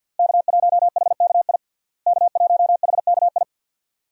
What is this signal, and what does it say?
Morse code, heard as a keyed tone, about 50 wpm, 700 Hz. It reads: K1HKI K1HKI
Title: USA Call Signs - 5 Characters - Repeated 2x - 50wpm